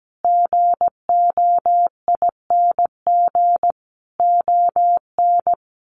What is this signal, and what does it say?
Morse code, heard as a keyed tone, about 17 wpm, 700 Hz. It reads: GOING ON
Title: Sets of 5 Words Encore2 17wpm